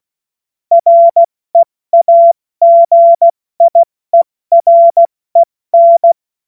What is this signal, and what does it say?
Morse code, heard as a keyed tone, about 16 wpm, 680 Hz. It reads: REAGIEREN
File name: Morsekurs / Witze / Witz 14. Dezember 2021, 70 F80